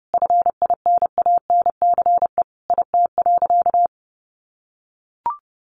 Morse code, heard as a keyed tone, about 30 wpm, 700 Hz. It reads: FINANCE IT.
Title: Sentences from Top 1500 Words 30wpm